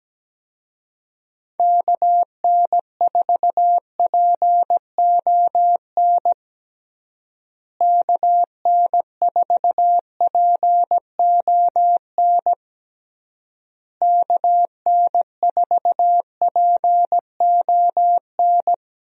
Morse code, heard as a keyed tone, about 17 wpm, 700 Hz. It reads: KN4PON KN4PON KN4PON
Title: USA Call Signs - 6 Characters - Repeated 3x - 17wpm